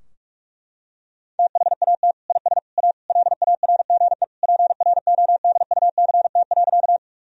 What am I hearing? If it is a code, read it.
Morse code, heard as a keyed tone, about 46 wpm, 700 Hz. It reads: THAT IS A LARGE PRODUCT.
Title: Sentences from Top 1000 Words - Review Part 2 - 45wpm